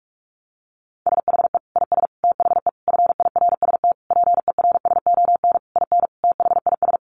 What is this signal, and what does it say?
Morse code, heard as a keyed tone, about 44 wpm, 700 Hz. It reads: SHE IS THE FIRST PERSON IN THIS
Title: Sentences from Top 400 Words 45wpm